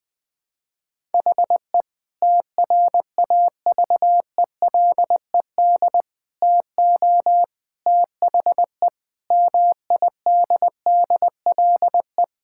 Morse code, heard as a keyed tone, about 21 wpm, 700 Hz. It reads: HE TRAVELED TO THE MIDDLE
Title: Sentences with 3-Word Phrases - 20wpm